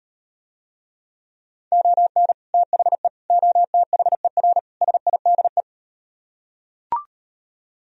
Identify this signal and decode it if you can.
Morse code, heard as a keyed tone, about 38 wpm, 700 Hz. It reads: ON THE OTHER SIDE